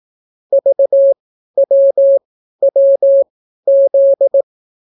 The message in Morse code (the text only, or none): VWWZ